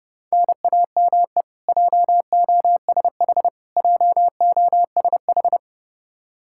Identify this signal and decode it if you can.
Morse code, heard as a keyed tone, about 30 wpm, 700 Hz. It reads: NAME JOSH JOSH
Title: Practice QSO - Exchange 1 - Beginning - 30wpm